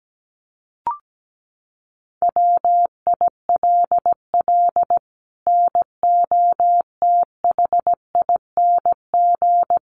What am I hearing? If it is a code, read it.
Morse code, heard as a keyed tone, about 17 wpm, 700 Hz. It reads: WILL NOTHING
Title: Sentences from Top 1200 Words 17wpm